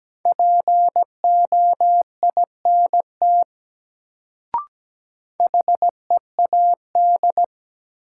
Morse code, heard as a keyed tone, about 17 wpm, 700 Hz. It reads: POINT HEAD
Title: Sets of 3 Words - New Word Drill - 17wpm